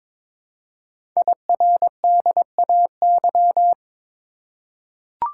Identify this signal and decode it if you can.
Morse code, heard as a keyed tone, about 22 wpm, 700 Hz. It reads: IRDAY
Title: Code Groups - 5 Letters - 22wpm